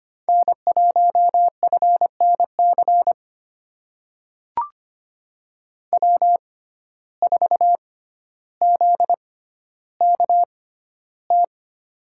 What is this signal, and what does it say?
Morse code, heard as a keyed tone, about 25 wpm, spaced as for 13 wpm, 700 Hz. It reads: N1FNC W4ZKT
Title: USA Call Signs - 5 Characters - 6x Character Spacing - 25wpm